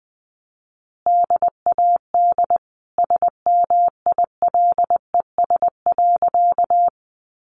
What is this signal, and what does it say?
Morse code, heard as a keyed tone, about 20 wpm, 700 Hz. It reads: DAD SMILES.
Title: Easy Phrases 20wpm